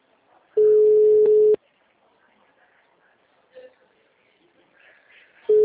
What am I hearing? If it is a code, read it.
Ringback tone (Europe, 425 Hz).